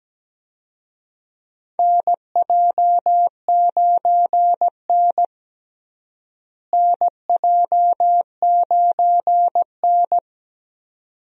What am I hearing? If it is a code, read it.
Morse code, heard as a keyed tone, about 17 wpm, 700 Hz. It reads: NJ9N NJ9N
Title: USA Call Signs - 4 Characters - Repeated 2x - 17wpm